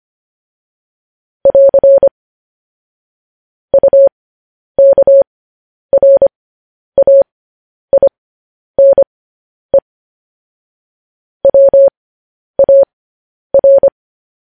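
Morse code, heard as a keyed tone, about 25 wpm, spaced as for 10 wpm, 550 Hz. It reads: <AR> UKRAINE WAR
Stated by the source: Morse code news headlines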